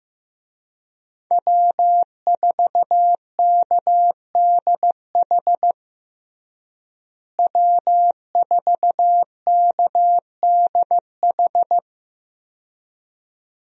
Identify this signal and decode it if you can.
Morse code, heard as a keyed tone, about 15 wpm, 700 Hz. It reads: W4KDH W4KDH
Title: USA Call Signs - 5 Characters - Repeated 2x - 15wpm